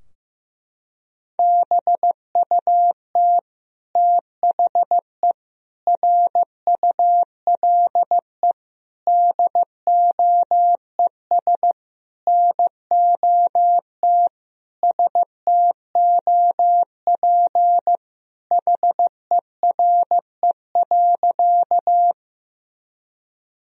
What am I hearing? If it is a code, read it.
Morse code, heard as a keyed tone, about 15 wpm, 700 Hz. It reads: BUT THE RULE DOES NOT STOP HERE.
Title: Sentences from Top 1000 Words - Review Part 2 - 15wpm